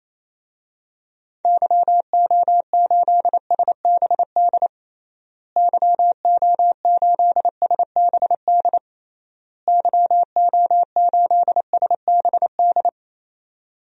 Morse code, heard as a keyed tone, about 28 wpm, 700 Hz. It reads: YO8SBD YO8SBD YO8SBD
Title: DXCC-100 Call Signs - 6 Characters - Repeated 3x - 28wpm